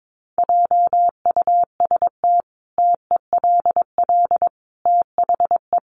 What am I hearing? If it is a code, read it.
Morse code, heard as a keyed tone, about 22 wpm, 700 Hz. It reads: JUST TELL THE
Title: Sentences from Top 800 Words 22wpm